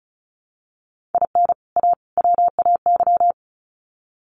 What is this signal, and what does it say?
Morse code, heard as a keyed tone, about 34 wpm, 700 Hz. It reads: IN A WAY
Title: Sets of 3 Words 35wpm